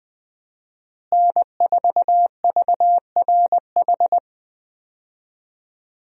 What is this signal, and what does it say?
Morse code, heard as a keyed tone, about 20 wpm, 700 Hz. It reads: N4VRH
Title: USA Call Signs - 5 Characters - Repeated 2x - 20wpm